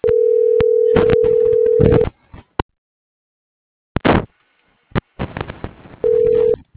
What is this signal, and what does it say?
Ringback tone (North America), 440 + 480 Hz.